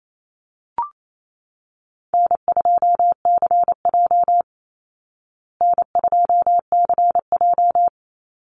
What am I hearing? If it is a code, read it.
Morse code, heard as a keyed tone, about 28 wpm, 700 Hz. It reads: N2CJ N2CJ
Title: Call Signs Repeated 3x - 28wpm